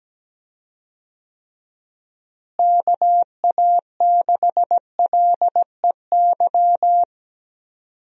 Morse code, heard as a keyed tone, about 17 wpm, 700 Hz. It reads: KA6LEY
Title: USA Call Signs - 6 Characters - Repeated 2x - 17wpm